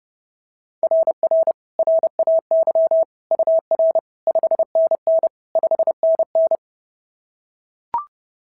Morse code, heard as a keyed tone, about 30 wpm, 690 Hz. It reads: RR RAY UR 5NN 5NN
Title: SOTA QSO Practice 30wpm